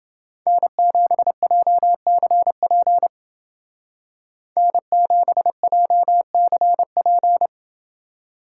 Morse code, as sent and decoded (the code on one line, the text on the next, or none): -. --... .--- -.-. .--. / -. --... .--- -.-. .--.
N7JCP N7JCP